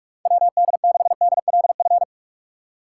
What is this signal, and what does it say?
Morse code, heard as a keyed tone, about 44 wpm, 700 Hz. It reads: WD6DLF
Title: USA Call Signs - 6 Characters - 3x Character Spacing - 45wpm